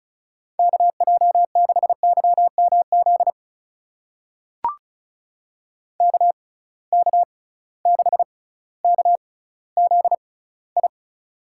Morse code, heard as a keyed tone, about 36 wpm, spaced as for 18 wpm, 700 Hz. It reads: KJ6YMZ KK6KZI
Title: USA Call Signs - 6 Characters - 6x Character Spacing - 35wpm